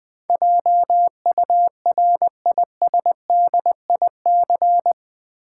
Morse code, heard as a keyed tone, about 20 wpm, 700 Hz. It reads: JURISDIC
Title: Build A Word - 4 Letters At A Time - 12-Letter Words - 20wpm